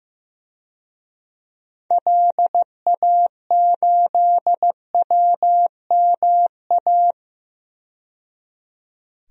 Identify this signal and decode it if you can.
Morse code, heard as a keyed tone, about 15 wpm, 700 Hz. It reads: LA8WMA
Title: DXCC-100 Call Signs - 6 Characters - 15wpm